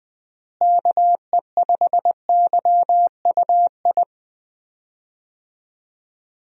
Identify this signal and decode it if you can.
Morse code, heard as a keyed tone, about 20 wpm, 700 Hz. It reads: KE5YUI
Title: USA Call Signs - 6 Characters - Repeated 2x - 20wpm